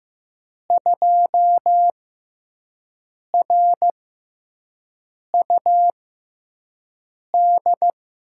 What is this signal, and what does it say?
Morse code, heard as a keyed tone, about 15 wpm, spaced as for 5 wpm, 700 Hz. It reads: 2RUD
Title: USA Call Signs - 5 Characters - 6x Character Spacing - 15wpm